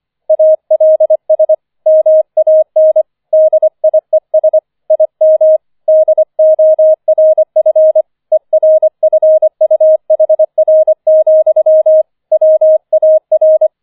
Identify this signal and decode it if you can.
Morse code, heard as a keyed tone, about 25 wpm, 630 Hz. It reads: ALS MAN DIES IM DORF ERFUHR, WAR